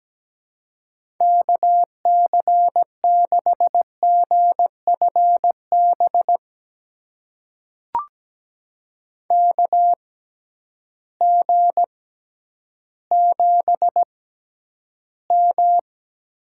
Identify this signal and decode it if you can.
Morse code, heard as a keyed tone, about 17 wpm, spaced as for 10 wpm, 700 Hz. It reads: KC6GFB KG7M
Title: USA Call Signs - 6 Characters - 6x Character Spacing - 17wpm